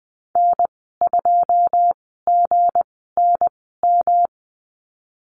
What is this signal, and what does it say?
Morse code, heard as a keyed tone, about 20 wpm, spaced as for 15 wpm, 700 Hz. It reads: N2GNM
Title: 167 Lesson 27g - Number 7 - Call Signs 20wpm